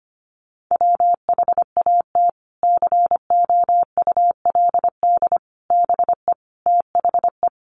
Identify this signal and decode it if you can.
Morse code, heard as a keyed tone, about 25 wpm, 700 Hz. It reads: WHAT COULD BE THE
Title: Sentences from Top 900 Words 25wpm